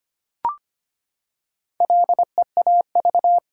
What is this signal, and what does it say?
Morse code, heard as a keyed tone, about 25 wpm, 700 Hz. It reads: LEAV
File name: practice / 25wpm / Sentences with 3-Word Phrases - 25wpm